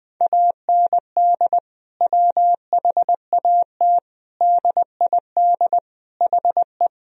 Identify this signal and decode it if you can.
Morse code, heard as a keyed tone, about 20 wpm, 700 Hz. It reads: AND WHAT DID HE
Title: Sentences from Top 1000 Words - Review Part 4 - 20wpm